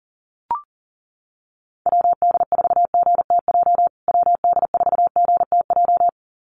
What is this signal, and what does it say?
Morse code, heard as a keyed tone, about 40 wpm, 700 Hz. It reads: WD4GTJ WD4GTJ
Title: USA Call Signs Repeated 2x 40wpm